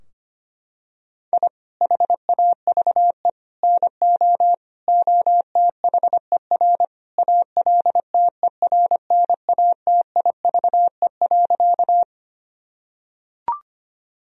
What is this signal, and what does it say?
Morse code, heard as a keyed tone, about 25 wpm, 700 Hz. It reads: I HAVE NO OTHER ALTERNATIVE.
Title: Sentences from Top 1000 Words - Review Part 4 - 25wpm